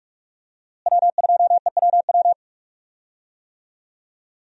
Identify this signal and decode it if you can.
Morse code, heard as a keyed tone, about 45 wpm, 700 Hz. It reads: W2EWW
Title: USA Call Signs - 5 Characters - Repeated 2x - 45wpm